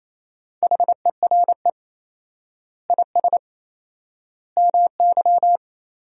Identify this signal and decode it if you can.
Morse code, heard as a keyed tone, about 28 wpm, 700 Hz. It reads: HERE IS MY
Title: 4x Word-Spacing - Sentences from Top 200 Words - 28wpm